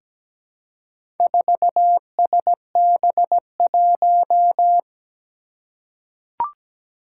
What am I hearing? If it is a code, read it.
Morse code, heard as a keyed tone, about 17 wpm, 700 Hz. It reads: 4SB1
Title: Four Challenging Letter-Number Combinations 17wpm